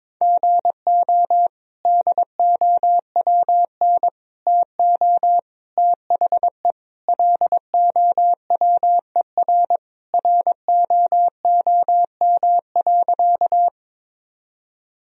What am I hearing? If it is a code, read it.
Morse code, heard as a keyed tone, about 22 wpm, 700 Hz. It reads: GO DOWN TO THE LOWER ROOM.
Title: Sentences from Top 1000 Words - Review Part 4 - 22wpm